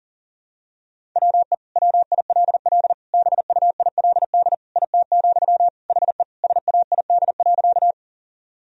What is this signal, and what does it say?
Morse code, heard as a keyed tone, about 40 wpm, 700 Hz. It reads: WE WILL BUILD IT, HE SAID.
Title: Sentences from Top 1000 Words - Review Part 1 - 40wpm